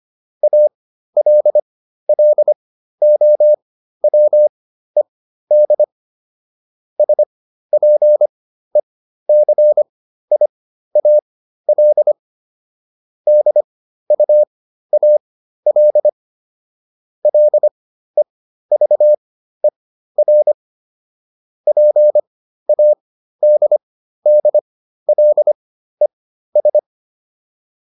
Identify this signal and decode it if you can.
Morse code, heard as a keyed tone, about 25 wpm, spaced as for 13 wpm, 600 Hz. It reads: ALLOWED SPECIAL DUAL LEVER PADDLES